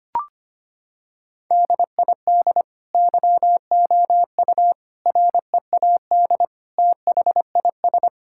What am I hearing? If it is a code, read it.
Morse code, heard as a keyed tone, about 25 wpm, 700 Hz. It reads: DID YOU READ THIS
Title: Sentences from Top 1000 Words - Review Part 5 - 25wpm